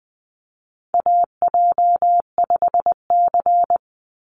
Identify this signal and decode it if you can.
Morse code, heard as a keyed tone, about 20 wpm, 700 Hz. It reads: AJ5C
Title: NA SOTA Chaser Callsigns 20wpm